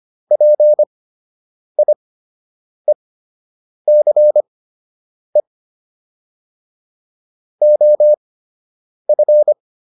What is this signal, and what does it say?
Morse code, heard as a keyed tone, about 25 wpm, spaced as for 8 wpm, 600 Hz. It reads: PIECE OF